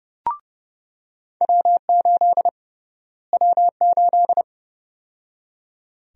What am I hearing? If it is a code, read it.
Morse code, heard as a keyed tone, about 31 wpm, 700 Hz. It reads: W8 W8
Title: USA Call Sign Prefixes - Repeated 2x - 30wpm